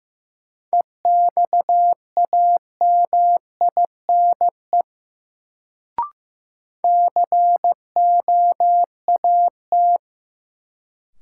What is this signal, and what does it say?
Morse code, heard as a keyed tone, about 15 wpm, 700 Hz. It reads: EXAMINE COAT
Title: Sentences from Top 1600 Words - New Word Drill - 15wpm